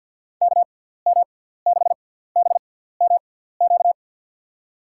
Morse code, heard as a keyed tone, about 52 wpm, spaced as for 17 wpm, 700 Hz. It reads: KM6BMQ
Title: USA Call Signs - 6 Characters - 6x Character Spacing - 50wpm